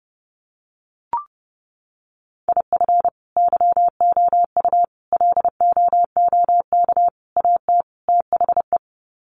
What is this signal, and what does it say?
Morse code, heard as a keyed tone, about 30 wpm, 700 Hz. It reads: IF YOU LOOK AT THE
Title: Sets of 5 Words 30wpm